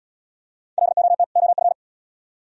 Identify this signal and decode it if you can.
Morse code, heard as a keyed tone, about 74 wpm, 700 Hz. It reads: LOT OF